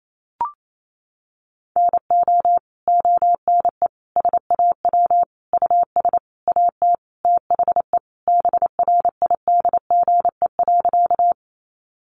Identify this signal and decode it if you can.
Morse code, heard as a keyed tone, about 28 wpm, 700 Hz. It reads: NO ONE SAW US AT THE BRIDGE.
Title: Sentences from Top 1200 Words 28wpm